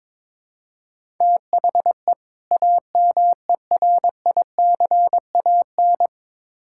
Morse code, heard as a keyed tone, about 22 wpm, 700 Hz. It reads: THE AMERICAN
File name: Build A Phrase - 2-Word Phrases - 22wpm